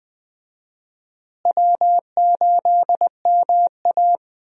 Morse code, heard as a keyed tone, about 21 wpm, 700 Hz. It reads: W8MA
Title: USA Call Signs - 4 Characters - Repeated 2x - 20wpm